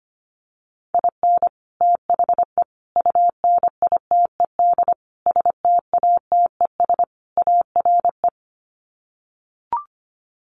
Morse code, heard as a keyed tone, about 25 wpm, 700 Hz. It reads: IN THE UNITED STATES ARE
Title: Sets of 5 Words Encore 25wpm